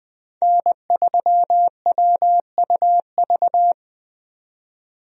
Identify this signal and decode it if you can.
Morse code, heard as a keyed tone, about 20 wpm, 700 Hz. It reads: N3WUV
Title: USA Call Signs - 5 Characters - 20wpm